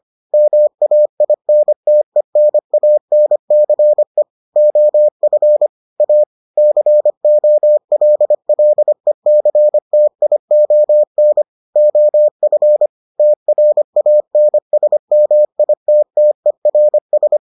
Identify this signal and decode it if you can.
Morse code, heard as a keyed tone, about 25 wpm, 600 Hz. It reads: MAINTENANCE OF A COLLECTION OF TRANSMITTERS